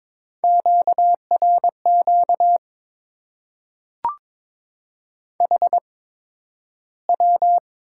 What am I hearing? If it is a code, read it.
Morse code, heard as a keyed tone, about 22 wpm, spaced as for 12 wpm, 700 Hz. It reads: QRQ HW
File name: Top 90 QSO Elements - 8x Character Spacing - 22wpm